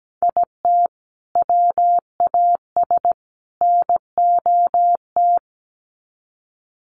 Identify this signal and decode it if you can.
Morse code, heard as a keyed tone, about 17 wpm, 700 Hz. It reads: IT WAS NOT
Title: Sets of 3 Words 17wpm